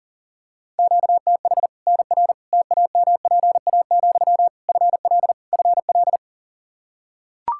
Morse code, heard as a keyed tone, about 40 wpm, 700 Hz. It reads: QTH NR TAMPA, FL FL
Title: K1USN SST QSO Practice 40wpm